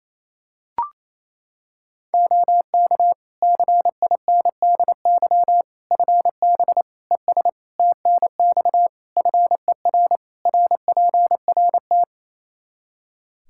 Morse code, heard as a keyed tone, about 28 wpm, 700 Hz. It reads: OK CINDY FB ES TNX FER RPRT
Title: K1USN SST QSO Practice 28wpm